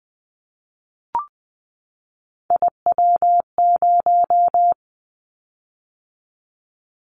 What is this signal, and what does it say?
Morse code, heard as a keyed tone, about 19 wpm, 700 Hz. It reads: IW0
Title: Callsign Prefixes for Top 50 DXCC - 20wpm